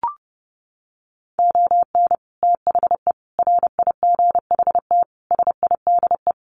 Morse code, heard as a keyed tone, about 30 wpm, 700 Hz. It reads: ON THE RIGHT SIDE
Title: Sets of 5 Words Encore 30wpm